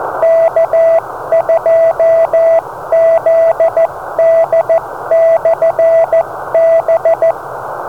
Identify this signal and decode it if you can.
Morse code, heard as a keyed tone, about 14 wpm, 660 Hz, with radio noise. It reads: K2ZD/B
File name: beacons / K2ZD